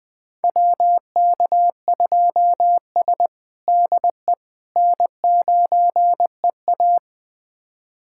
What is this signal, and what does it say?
Morse code, heard as a keyed tone, about 20 wpm, 700 Hz. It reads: WK2S DE N9EA
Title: USA Callsign DE Callsign - 4 Characters - 20wpm